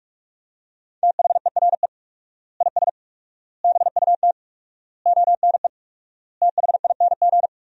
Morse code, heard as a keyed tone, about 44 wpm, 700 Hz. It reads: THERE IS BUT ONE THING
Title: 4x Word-Spacing - Sentences from Top 100 Words - 45wpm